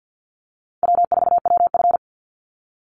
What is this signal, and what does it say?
Morse code, heard as a keyed tone, about 50 wpm, 700 Hz. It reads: W4WF
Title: Call Signs - 50wpm